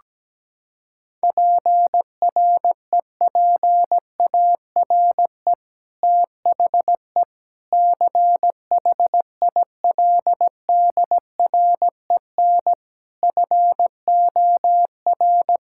Morse code, heard as a keyed tone, about 17 wpm, 700 Hz. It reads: PREPARE THE CHILDREN FOR
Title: Sentences from Top 1000 Words - Review Part 6 - 17wpm